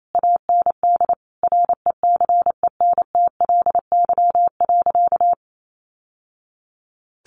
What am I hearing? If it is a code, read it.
Morse code, heard as a keyed tone, about 28 wpm, 700 Hz. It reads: AND RECENTLY.
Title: Sentences from Top 1200 Words 28wpm